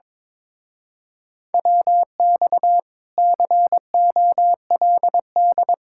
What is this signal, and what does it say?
Morse code, heard as a keyed tone, about 22 wpm, 700 Hz. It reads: WX COLD
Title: Speed-Racing Common QSO Phrases - 22wpm